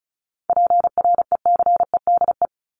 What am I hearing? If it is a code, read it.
Morse code, heard as a keyed tone, about 34 wpm, 700 Hz. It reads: PRECEDE
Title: ING Suffix - Top 100 Words with Base Words 35wpm